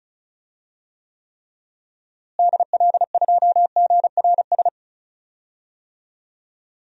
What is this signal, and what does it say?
Morse code, heard as a keyed tone, about 34 wpm, 700 Hz. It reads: DL2GRS
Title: DXCC-100 Call Signs - 6 Characters - Repeated 3x - 35wpm